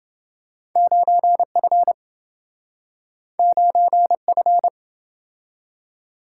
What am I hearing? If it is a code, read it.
Morse code, heard as a keyed tone, about 27 wpm, 700 Hz. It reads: 9F 9F